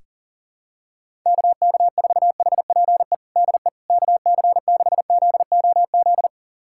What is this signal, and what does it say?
Morse code, heard as a keyed tone, about 40 wpm, 700 Hz. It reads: KK4HPE DE KC6ZOZ
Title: USA Callsign DE Callsign - 6 Characters - 40wpm